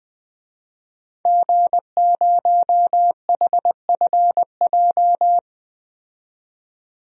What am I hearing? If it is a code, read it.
Morse code, heard as a keyed tone, about 20 wpm, 700 Hz. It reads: G0HFJ
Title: DXCC-100 Call Signs - 5 Characters - 20wpm